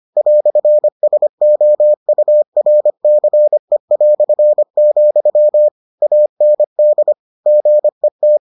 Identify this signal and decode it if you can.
Morse code, heard as a keyed tone, about 25 wpm, 600 Hz. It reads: "SOURCE", AND GET